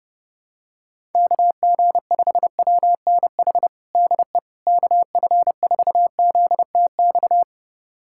Morse code, heard as a keyed tone, about 30 wpm, 700 Hz. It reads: KG5WNH DE KF4ZTX
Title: USA Callsign DE Callsign 30wpm